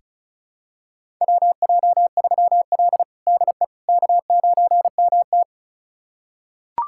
Morse code, heard as a keyed tone, about 36 wpm, 700 Hz. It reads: WJ3L DE K9MT
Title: USA Callsign DE Callsign - 4 Characters - 35wpm